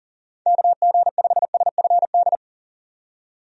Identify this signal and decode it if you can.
Morse code, heard as a keyed tone, about 40 wpm, 700 Hz. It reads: KG5SFD
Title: USA Call Signs - 6 Characters - 40wpm